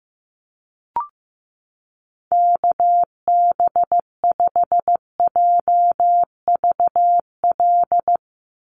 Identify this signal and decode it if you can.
Morse code, heard as a keyed tone, about 15 wpm, 700 Hz. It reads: KB5JVL
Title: USA Call Signs 15wpm